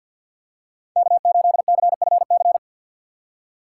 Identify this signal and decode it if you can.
Morse code, heard as a keyed tone, about 47 wpm, 700 Hz. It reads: K8CFC
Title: USA Call Signs - 5 Characters - 4x Character Spacing - 50wpm